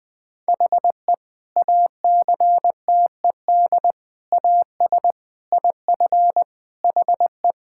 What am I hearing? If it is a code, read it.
Morse code, heard as a keyed tone, about 21 wpm, 700 Hz. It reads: HE ACTED AS IF HE
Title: Sentences with 4-Word Phrases - Spoken Beginning - 20wpm